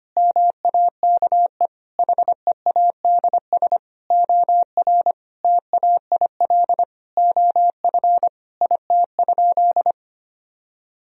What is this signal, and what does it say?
Morse code, heard as a keyed tone, about 25 wpm, 700 Hz. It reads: MAKE HEADS OR TAIL OF IT?
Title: Sentences from Top 1000 Words - Review Part 3 - 25wpm